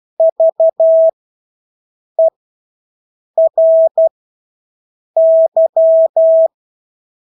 Morse code, heard as a keyed tone, about 12 wpm, spaced as for 6 wpm, 650 Hz. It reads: VERY
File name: Engelske ord 12-6 wpm